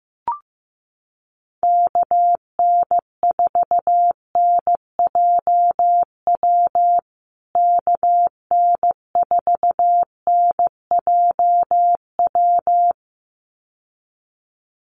Morse code, heard as a keyed tone, about 15 wpm, 700 Hz. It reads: KN4NJW KN4NJW
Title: USA Call Signs Repeated 2x 15wpm